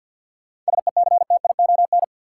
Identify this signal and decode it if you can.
Morse code, heard as a keyed tone, about 47 wpm, 700 Hz. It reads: SECTION
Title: Sentences from Top 1000 Words - Review Part 5 - New Word Drill - 50wpm